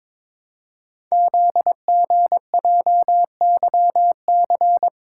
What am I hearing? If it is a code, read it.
Morse code, heard as a keyed tone, about 22 wpm, 700 Hz. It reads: ZGJYC
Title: Code Groups - 5 Letter:Numbers - 22wpm